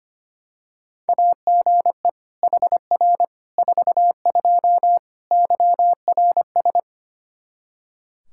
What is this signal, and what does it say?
Morse code, heard as a keyed tone, about 25 wpm, 700 Hz. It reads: AGE HR 42 YRS
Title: Practice QSO - Exchange 3 - More Info - 25wpm